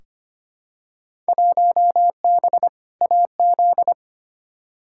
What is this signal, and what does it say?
Morse code, heard as a keyed tone, about 25 wpm, 700 Hz. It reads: 1B AZ
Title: Field Day Exchanges - 25wpm